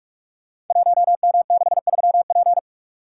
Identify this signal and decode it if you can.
Morse code, heard as a keyed tone, about 44 wpm, 690 Hz. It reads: 1M63P
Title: Code Groups - 5 Letter:Numbers - 45wpm